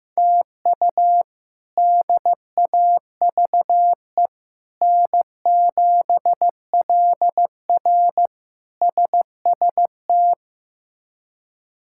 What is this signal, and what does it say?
Morse code, heard as a keyed tone, about 15 wpm, 700 Hz. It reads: TU DAVE N7LR SST
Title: K1USN SST QSO Practice - 15wpm